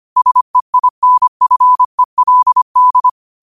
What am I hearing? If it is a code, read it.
Morse code, heard as a keyed tone, about 25 wpm, 1000 Hz. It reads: SEINFELD
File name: qotdfiles / 25wpm